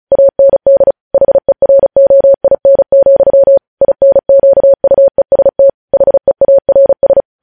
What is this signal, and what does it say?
Morse code, heard as a keyed tone, about 36 wpm, 550 Hz. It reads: AND HEROIN, INQUEST HEARS
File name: podcast / News headlines-1x-35x35